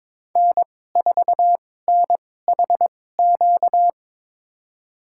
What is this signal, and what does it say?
Morse code, heard as a keyed tone, about 22 wpm, spaced as for 16 wpm, 700 Hz. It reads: N4NHQ
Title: USA Call Signs - 2x Character Spacing - 22wpm